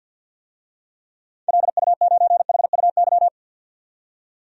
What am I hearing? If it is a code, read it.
Morse code, heard as a keyed tone, about 51 wpm, 700 Hz. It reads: LU9HUY
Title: DXCC-100 Call Signs - 6 Characters - 50wpm